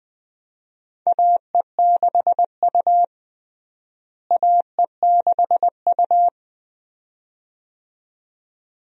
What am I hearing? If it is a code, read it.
Morse code, heard as a keyed tone, about 20 wpm, 700 Hz. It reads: AE6U AE6U
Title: USA Call Signs - 4 Characters - Repeated 2x - 20wpm